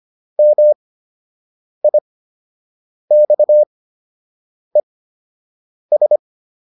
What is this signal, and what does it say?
Morse code, heard as a keyed tone, about 25 wpm, spaced as for 7 wpm, 600 Hz. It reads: MIXES